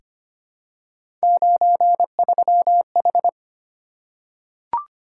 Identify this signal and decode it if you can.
Morse code, heard as a keyed tone, about 25 wpm, 700 Hz. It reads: 93H
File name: Three Letter-Number Combinations - 25wpm